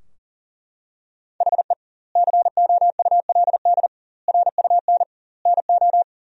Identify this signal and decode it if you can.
Morse code, heard as a keyed tone, about 40 wpm, 700 Hz. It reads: HE COULD RUN NO
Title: 2x Word-Spacing - Sentences from Top 300 Words - 40wpm